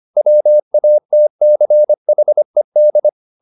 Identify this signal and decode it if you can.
Morse code, heard as a keyed tone, about 25 wpm, 600 Hz. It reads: WATCHED